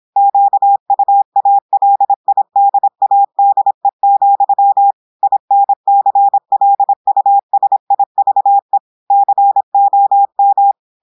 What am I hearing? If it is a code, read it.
Morse code, heard as a keyed tone, about 26 wpm, 800 Hz. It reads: QUALIDADE, INCLUSIVE COM
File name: portugais texte en clair 2 vitesse 26 mots minutes